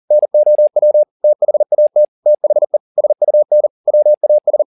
Morse code, heard as a keyed tone, about 40 wpm, 600 Hz. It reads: NOW THAT THE SUN WAS